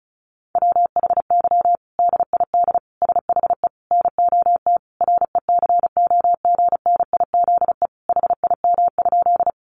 Morse code, heard as a keyed tone, about 34 wpm, 700 Hz. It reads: WHY DID SHE NOT RECOGNIZE HIM?
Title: Sentences from Top 1400 Words 35wpm